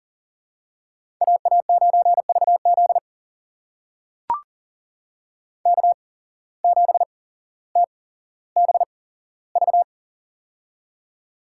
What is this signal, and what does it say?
Morse code, heard as a keyed tone, about 39 wpm, spaced as for 17 wpm, 700 Hz. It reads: AA9VZ K7TBV
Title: USA Call Signs - 5 Characters - 8x Character Spacing - 40wpm